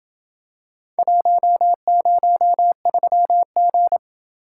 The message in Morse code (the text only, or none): 103G